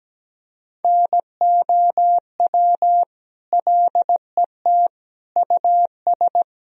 Morse code, heard as a keyed tone, about 17 wpm, 700 Hz. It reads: NOW LET US
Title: Sentences from Top 1000 Words - Review Part 1 - 17wpm